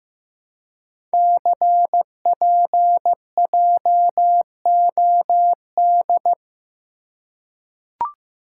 Morse code, heard as a keyed tone, about 15 wpm, 700 Hz. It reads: CPJOD
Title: Code Groups - 5 Letters - 15wpm